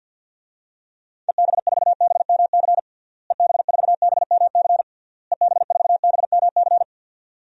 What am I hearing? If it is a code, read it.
Morse code, heard as a keyed tone, about 48 wpm, 700 Hz. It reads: EB4BMC EB4BMC EB4BMC
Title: DXCC-100 Call Signs - 6 Characters - Repeated 3x - 50wpm